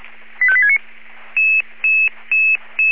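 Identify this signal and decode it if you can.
Five-tone selective call (ZVEI), heard as quick runs of five tones, tones of about 70 ms. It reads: 85869
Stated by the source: ZVEI five-tone call 85220 received on SDR.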